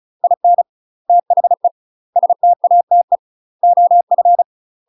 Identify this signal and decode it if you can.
Morse code, heard as a keyed tone, about 36 wpm, 700 Hz. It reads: IN THE STATE OF
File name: UK MASTER QSOs TRAINING / 35WPM / UK MASTER QSOs 0020